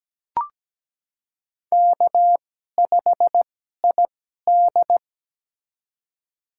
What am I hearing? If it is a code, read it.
Morse code, heard as a keyed tone, about 17 wpm, spaced as for 12 wpm, 700 Hz. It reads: K5ID
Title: USA Call Signs - 4 Characters - 2x Character Spacing - 17wpm